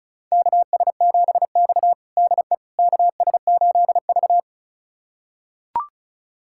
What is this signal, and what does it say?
Morse code, heard as a keyed tone, about 36 wpm, 700 Hz. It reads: KS7X DE KS8V
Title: USA Callsign DE Callsign - 4 Characters - 35wpm